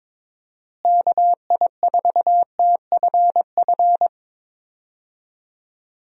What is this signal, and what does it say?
Morse code, heard as a keyed tone, about 22 wpm, 700 Hz. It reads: KI4TFF
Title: USA Call Signs - 6 Characters - Repeated 3x - 22wpm